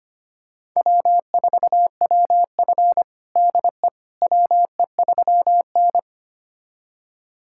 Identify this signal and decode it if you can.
Morse code, heard as a keyed tone, about 25 wpm, 700 Hz. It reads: W4WF DE WE3N
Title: USA Callsign DE Callsign - 4 Characters - 25wpm